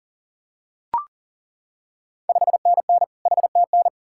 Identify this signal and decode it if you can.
Morse code, heard as a keyed tone, about 40 wpm, 700 Hz. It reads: 5NN HTN
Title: Ontario QSO Party Exchange 40wpm